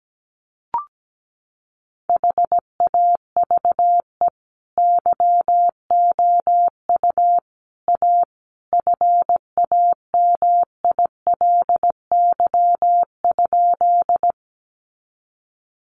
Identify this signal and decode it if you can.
Morse code, heard as a keyed tone, about 17 wpm, 700 Hz. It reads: HAVE YOU A FAMILY?
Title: Sentences from Top 400 Words 17wpm